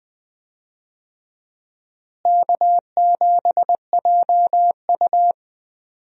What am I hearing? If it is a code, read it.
Morse code, heard as a keyed tone, about 20 wpm, 700 Hz. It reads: K7JU
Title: USA Call Signs - 4 Characters - Repeated 2x - 20wpm